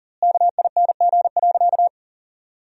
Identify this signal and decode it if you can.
Morse code, heard as a keyed tone, about 40 wpm, 700 Hz. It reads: KING.
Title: 3x Word-Spacing - Sentences from Top 400 Words - 40wpm